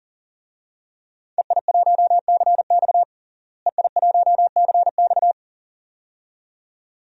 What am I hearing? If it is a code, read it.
Morse code, heard as a keyed tone, about 40 wpm, 700 Hz. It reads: EI1CX EI1CX
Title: DXCC-100 Call Signs - 5 Characters - Repeated 2x - 40wpm